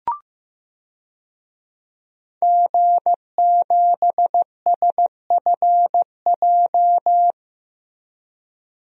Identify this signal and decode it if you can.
Morse code, heard as a keyed tone, about 16 wpm, 700 Hz. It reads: G7SFJ
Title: DXCC-100 Call Signs - 5 Characters - 15wpm